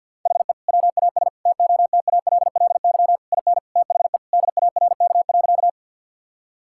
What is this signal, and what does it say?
Morse code, heard as a keyed tone, about 51 wpm, 700 Hz. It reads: HE WAS TOTALLY IN THE DARK.
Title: Sentences from Top 1600 Words 50wpm